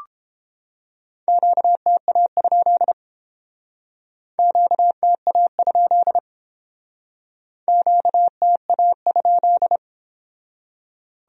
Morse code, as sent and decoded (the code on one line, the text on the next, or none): --.- - .- ..--.. / --.- - .- ..--.. / --.- - .- ..--..
QTA? QTA? QTA?